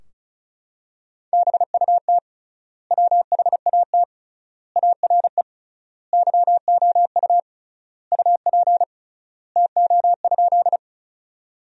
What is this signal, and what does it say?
Morse code, heard as a keyed tone, about 36 wpm, 700 Hz. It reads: BUT WHAT ARE YOU UP TO?
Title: 3x Word-Spacing - Sentences from Top 100 Words - 35wpm